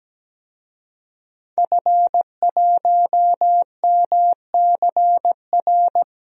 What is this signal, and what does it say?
Morse code, heard as a keyed tone, about 17 wpm, 700 Hz. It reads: F1MCR
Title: DXCC-100 Call Signs - 5 Characters - Repeated 2x - 17wpm